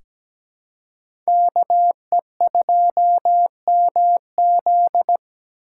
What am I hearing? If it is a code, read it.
Morse code, heard as a keyed tone, about 17 wpm, 700 Hz. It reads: KE2MZ
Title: USA Call Signs - 5 Characters - 2x Character Spacing - 17wpm